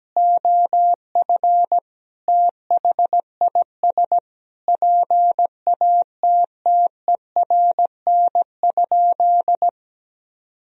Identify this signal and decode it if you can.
Morse code, heard as a keyed tone, about 17 wpm, 700 Hz. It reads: OF THIS PATTERN?
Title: Sentences from Top 1000 Words - Review Part 2 - 17wpm